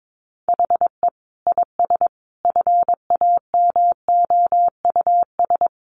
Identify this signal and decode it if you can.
Morse code, heard as a keyed tone, about 22 wpm, 700 Hz. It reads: HE IS FAMOUS
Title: Sentences from Top 1500 Words 22wpm